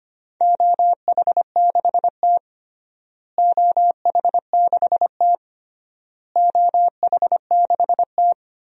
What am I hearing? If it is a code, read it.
Morse code, heard as a keyed tone, about 25 wpm, 700 Hz. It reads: OH6T OH6T OH6T
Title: DXCC-100 Call Signs - 4 Characters - Repeated 3x - 25wpm